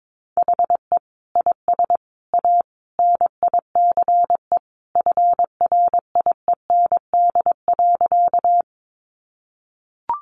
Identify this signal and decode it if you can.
Morse code, heard as a keyed tone, about 22 wpm, 700 Hz. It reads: HE IS A NICE FRIEND.
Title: Sentences from Top 1100 Words 22wpm